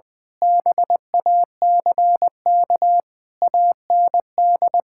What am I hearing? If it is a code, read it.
Morse code, heard as a keyed tone, about 20 wpm, 700 Hz. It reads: BACK AND
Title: Sentences with 6-Word Phrases - 20wpm